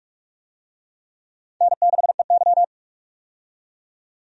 Morse code, heard as a keyed tone, about 45 wpm, 700 Hz. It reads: N6EY
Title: USA Call Signs - 4 Characters - 45wpm